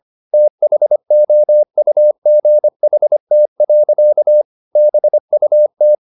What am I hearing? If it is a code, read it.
Morse code, heard as a keyed tone, about 25 wpm, 600 Hz. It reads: THOUGHT. BUT
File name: the old man and the sea part 1 25wpm